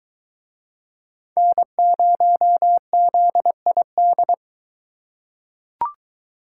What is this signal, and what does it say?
Morse code, heard as a keyed tone, about 23 wpm, 700 Hz. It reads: N0ZID